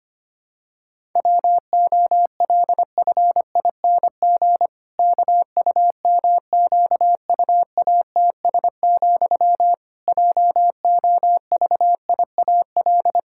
Morse code, heard as a keyed tone, about 25 wpm, 700 Hz. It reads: WOLFING KUMQUATS, JOVIAL
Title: Pangram Challenge v2 - 25wpm